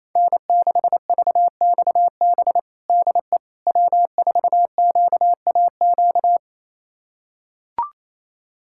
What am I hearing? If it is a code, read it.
Morse code, heard as a keyed tone, about 28 wpm, 700 Hz. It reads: N6VXB DE W4QAQ
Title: USA Callsign DE Callsign - 5 Characters - 28wpm